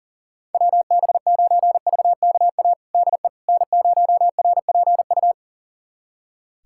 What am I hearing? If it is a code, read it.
Morse code, heard as a keyed tone, about 37 wpm, 700 Hz. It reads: WB9VKA DE N0RPU
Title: USA Callsign DE Callsign 40wpm